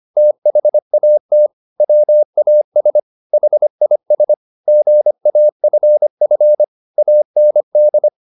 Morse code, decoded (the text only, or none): THAT WAS HIS GAFF AND